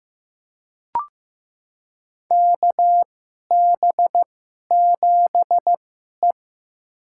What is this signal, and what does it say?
Morse code, heard as a keyed tone, about 15 wpm, spaced as for 11 wpm, 700 Hz. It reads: KB7E
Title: USA Call Signs - 4 Characters - 2x Character Spacing - 15wpm